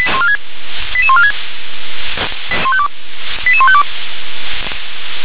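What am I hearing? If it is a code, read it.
Five-tone selective call (ZVEI), heard as quick runs of five tones, tones of about 70 ms. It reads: 88146 88152
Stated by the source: Five-tone ZVEI selective call received by SDR.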